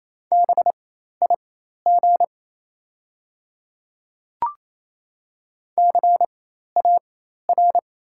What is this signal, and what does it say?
Morse code, heard as a keyed tone, about 28 wpm, spaced as for 13 wpm, 700 Hz. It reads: BIG CAR
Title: 4x Character Spacing - Top 300 Words - 28wpm